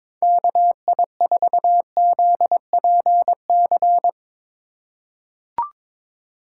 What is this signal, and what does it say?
Morse code, heard as a keyed tone, about 22 wpm, 700 Hz. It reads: KI4ZPC
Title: USA Call Signs - 6 Characters - 2x Character Spacing - 22wpm